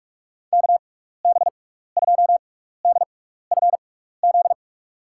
Morse code, heard as a keyed tone, about 46 wpm, spaced as for 16 wpm, 700 Hz. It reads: KB2DFZ
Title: USA Call Signs - 6 Characters - 6x Character Spacing - 45wpm